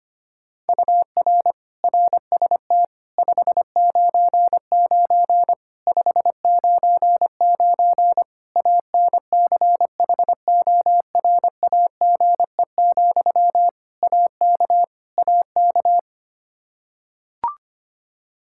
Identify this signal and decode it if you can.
Morse code, heard as a keyed tone, about 25 wpm, 700 Hz. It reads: UR RST 599 599 ANCHORAGE, AK AK
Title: Practice QSO - Exchange 1 - Beginning - 25wpm